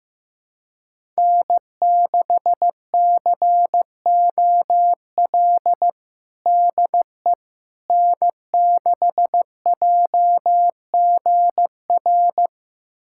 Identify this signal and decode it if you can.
Morse code, heard as a keyed tone, about 16 wpm, 700 Hz. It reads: N6COL DE N6JGR
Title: USA Callsign DE Callsign - 5 Characters - 15wpm